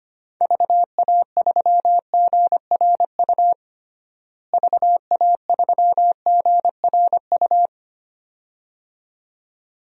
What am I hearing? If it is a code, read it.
Morse code, heard as a keyed tone, about 25 wpm, 700 Hz. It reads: VA3GRU VA3GRU
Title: DXCC-100 Call Signs - 6 Characters - Repeated 2x - 25wpm